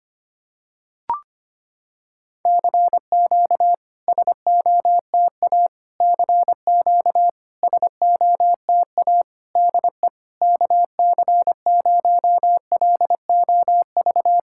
Morse code, heard as a keyed tone, about 25 wpm, 700 Hz. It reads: CQ SOTA CQ SOTA DE KC0LOV
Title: SOTA QSO Practice 25wpm